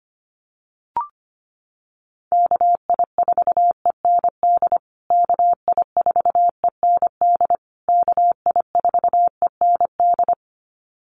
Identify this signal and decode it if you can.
Morse code, heard as a keyed tone, about 25 wpm, 700 Hz. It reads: KI4END KI4END KI4END
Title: USA Call Signs Repeated 3x 25wpm